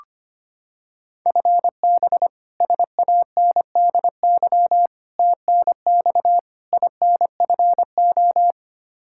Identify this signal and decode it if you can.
Morse code, heard as a keyed tone, about 25 wpm, 700 Hz. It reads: FB SANDY TNX INFO
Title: Practice QSO - Exchange 3 - More Info - 25wpm